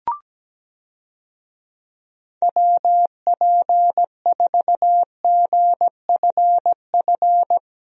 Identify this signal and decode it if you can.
Morse code, heard as a keyed tone, about 17 wpm, 700 Hz. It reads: WP4GFF
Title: USA Call Signs - 6 Characters - Repeated 3x - 17wpm